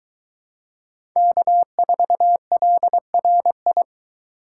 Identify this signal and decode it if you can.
Morse code, heard as a keyed tone, about 23 wpm, 700 Hz. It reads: K4LRI